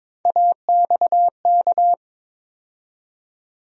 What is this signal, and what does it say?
Morse code, heard as a keyed tone, about 22 wpm, 700 Hz. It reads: AXK
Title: USA Call Sign Postfixes - 3 Letters 22wpm